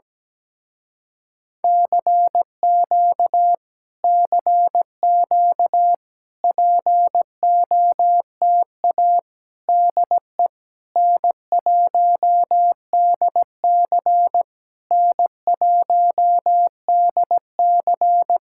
Morse code, heard as a keyed tone, about 17 wpm, 700 Hz. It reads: CQ CQ POTA DE N1DC N1DC
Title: Speed-Racing Common QSO Phrases - 17wpm